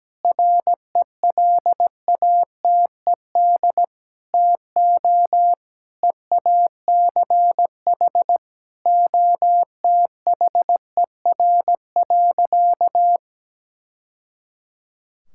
Morse code, heard as a keyed tone, about 17 wpm, 700 Hz. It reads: RELATED TO EACH OTHER.
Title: Sentences from Top 1000 Words - Review Part 5 - 17wpm